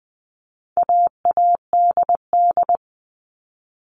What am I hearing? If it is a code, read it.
Morse code, heard as a keyed tone, about 20 wpm, 700 Hz. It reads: AADD
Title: All Cut Numbers - 4-Digit 20wpm